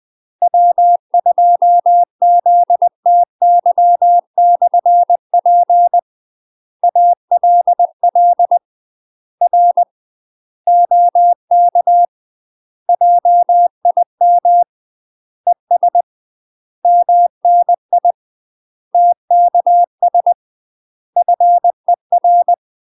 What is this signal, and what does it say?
Morse code, heard as a keyed tone, about 20 wpm, 700 Hz. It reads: W2ZTY/P ALL R OK JIM ES MNI TKS FER
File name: UK MASTER QSOs TRAINING / 20WPM / UK MASTER QSOs 0041